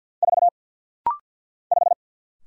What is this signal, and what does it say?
Morse code, heard as a keyed tone, about 48 wpm, 700 Hz. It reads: 4 5
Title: Single Letter-Number - Rapid-Fire 50wpm